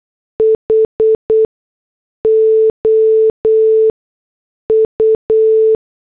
Morse code, heard as a keyed tone, about 8 wpm, spaced as for 6 wpm, 440 Hz. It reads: HOU
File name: morse / headlines